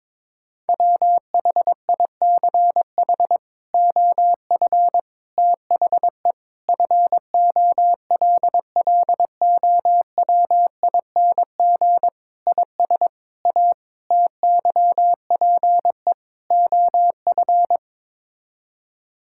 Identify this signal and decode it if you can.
Morse code, heard as a keyed tone, about 22 wpm, 700 Hz. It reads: WHICH OF THE FOLLOWING IS A TYPE OF
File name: practice / 22wpm / Sets of 8 Words - 22wpm